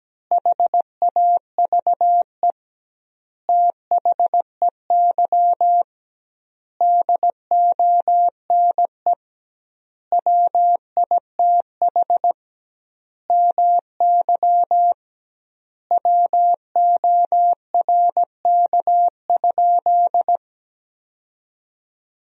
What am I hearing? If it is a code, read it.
Morse code, heard as a keyed tone, about 17 wpm, 700 Hz. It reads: HAVE THEY DONE WITH MY WORK?
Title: 2x Word-Spacing - Sentences from Top 500 Words - 17wpm